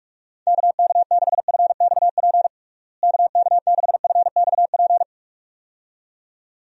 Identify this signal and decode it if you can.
Morse code, heard as a keyed tone, about 46 wpm, 700 Hz. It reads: KK6FXP KK6FXP
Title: USA Call Signs - 6 Characters - Repeated 2x - 45wpm